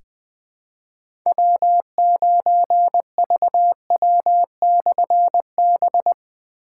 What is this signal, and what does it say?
Morse code, heard as a keyed tone, about 20 wpm, 700 Hz. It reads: W9VW/B
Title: Difficult Call Signs 20wpm